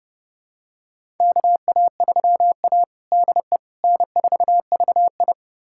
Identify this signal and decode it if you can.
Morse code, heard as a keyed tone, about 32 wpm, 700 Hz. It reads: KA3A DE N4VI
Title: USA Callsign DE Callsign - 4 Characters - 30wpm